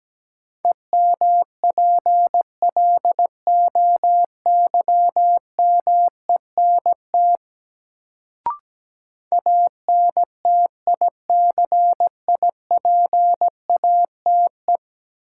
Morse code, heard as a keyed tone, about 17 wpm, 700 Hz. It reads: EMPLOYMENT ANTICIPATE
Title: Sentences from Top 1500 Words - New Word Drill - 17wpm